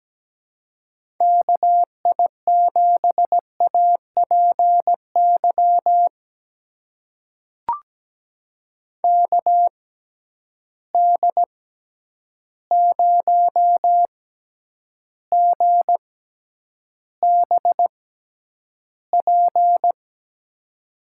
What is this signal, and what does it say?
Morse code, heard as a keyed tone, about 17 wpm, spaced as for 9 wpm, 700 Hz. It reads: KI7APY KD0GBP
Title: USA Call Signs - 6x Character Spacing - 17wpm